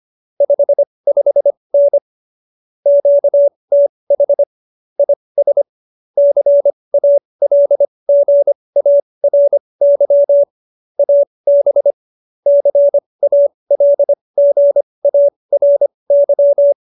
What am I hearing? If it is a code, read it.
Morse code, heard as a keyed tone, about 25 wpm, spaced as for 20 wpm, 570 Hz. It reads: 55N QTH IS CALGARY AB CALGARY
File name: QSO303 20